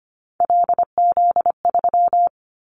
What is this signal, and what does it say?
Morse code, heard as a keyed tone, about 25 wpm, 700 Hz. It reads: LZ3
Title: Callsign Prefixes for Top 100 DXCC - 25wpm